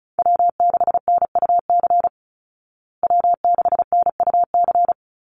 Morse code, heard as a keyed tone, about 34 wpm, 700 Hz. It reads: W6NUC W6NUC
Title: Call Signs Repeated 3x - 35wpm